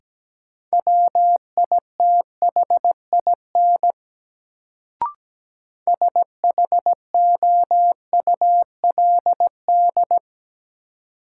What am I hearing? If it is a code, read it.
Morse code, heard as a keyed tone, about 17 wpm, 700 Hz. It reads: WITHIN SHOULD
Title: Sets of 2 Words - New Word Drill - 17wpm